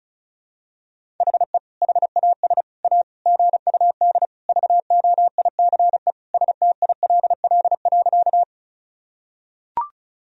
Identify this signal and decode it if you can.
Morse code, heard as a keyed tone, about 36 wpm, 700 Hz. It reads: HE HAS A GUD VOICE STILL.
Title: Sentences from Top 500 Words and Abbreviations 35wpm